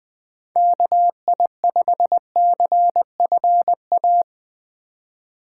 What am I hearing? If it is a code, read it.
Morse code, heard as a keyed tone, about 22 wpm, 700 Hz. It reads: KI5CFA
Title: Speed Racing - USA 2x3 Call Signs 20wpm